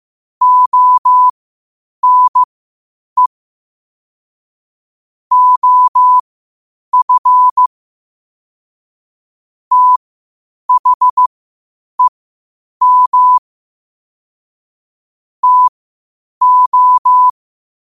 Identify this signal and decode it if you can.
Morse code, heard as a keyed tone, about 15 wpm, spaced as for 8 wpm, 1000 Hz. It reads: ONE OF THEM TO
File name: qotdfiles / 8wpm